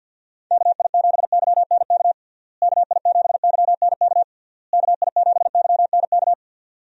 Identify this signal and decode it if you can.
Morse code, heard as a keyed tone, about 50 wpm, 700 Hz. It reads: KI7YNK KI7YNK KI7YNK
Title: USA Call Signs - 6 Characters - Repeated 3x - 50wpm